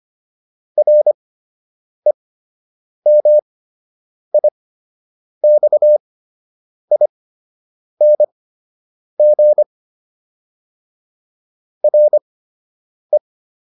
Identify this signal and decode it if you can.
Morse code, heard as a keyed tone, about 25 wpm, spaced as for 8 wpm, 600 Hz. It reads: REMIXING RE